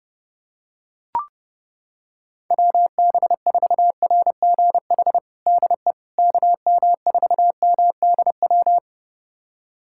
Morse code, heard as a keyed tone, about 32 wpm, 700 Hz. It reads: WB4RGH DE KM4MDW
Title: USA Callsign DE Callsign 30wpm